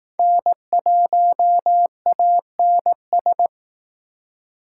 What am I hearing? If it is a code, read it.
Morse code, heard as a keyed tone, about 19 wpm, 700 Hz. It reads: N1ANS
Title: Speed Racing - USA 1x3 Call Signs 15wpm